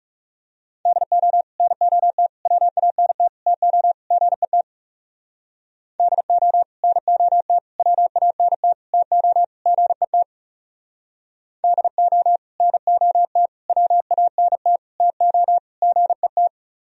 Morse code, heard as a keyed tone, about 36 wpm, 700 Hz. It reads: DO NOT WANT TO GET DO NOT WANT TO GET DO NOT WANT TO GET
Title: Speed-Racing - Sets of 5 Words 30wpm